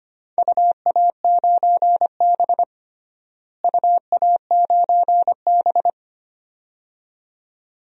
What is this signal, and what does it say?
Morse code, heard as a keyed tone, about 25 wpm, 700 Hz. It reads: UA9B UA9B
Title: DXCC-100 Call Signs - 4 Characters - Repeated 2x - 25wpm